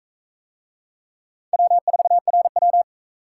Morse code, heard as a keyed tone, about 40 wpm, 700 Hz. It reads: W4RW